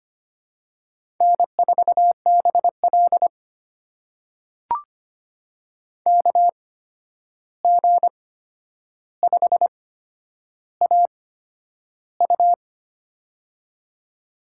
Morse code, heard as a keyed tone, about 25 wpm, spaced as for 10 wpm, 700 Hz. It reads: N4BL KG5AU
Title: USA Call Signs - 8x Character Spacing - 25wpm